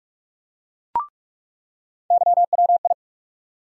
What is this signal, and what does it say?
Morse code, heard as a keyed tone, about 46 wpm, 700 Hz. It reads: YWI
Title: USA Call Sign Postfixes - 3 Letters 45wpm